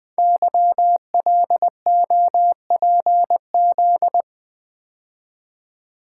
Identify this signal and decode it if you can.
Morse code, heard as a keyed tone, about 20 wpm, 700 Hz. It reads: YLOPZ
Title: Code Groups - 5 Letters - 20wpm